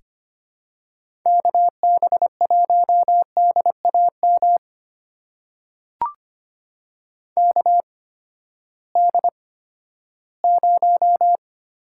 Morse code, heard as a keyed tone, about 25 wpm, spaced as for 14 wpm, 700 Hz. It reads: KB1DAM KD0
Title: USA Call Signs - 6 Characters - 8x Character Spacing - 25wpm